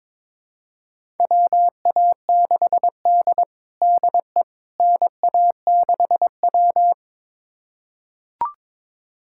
Morse code, heard as a keyed tone, about 22 wpm, 700 Hz. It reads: WA6D DE NA6W
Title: USA Callsign DE Callsign - 4 Characters - 22wpm